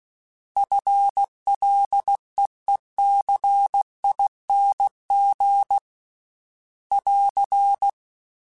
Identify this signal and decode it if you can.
Morse code, heard as a keyed tone, about 16 wpm, 800 Hz. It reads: FLEECING <AR>
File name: FL / FL mp3s / FL Prefixes 3x2 sp @15